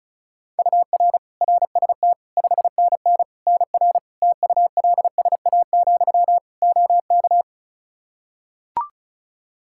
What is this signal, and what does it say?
Morse code, heard as a keyed tone, about 36 wpm, 700 Hz. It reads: UR RST 5NN NR TULSA, OK
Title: Practice QSO - Exchange 1 - Beginning - 35wpm